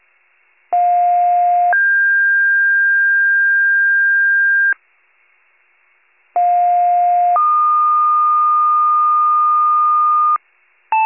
Two-tone page, heard as a one-second tone, then a three-second tone, about 710 then 1600 Hz, about 710 then 1220 Hz. It is heard from 0.7 to 10.4 s.